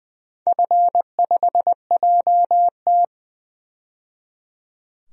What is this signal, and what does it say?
Morse code, heard as a keyed tone, about 21 wpm, 700 Hz. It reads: F5JT
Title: DXCC-100 Call Signs - 4 Characters - 20wpm